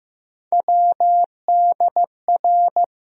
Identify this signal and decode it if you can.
Morse code, heard as a keyed tone, about 15 wpm, 700 Hz. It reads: WDR
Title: USA Call Sign Postfixes - 3 Letters 15wpm